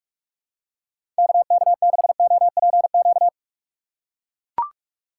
Morse code, heard as a keyed tone, about 44 wpm, 700 Hz. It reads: KK6OPQ
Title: USA Call Signs - 6 Characters - 6x Character Spacing - 45wpm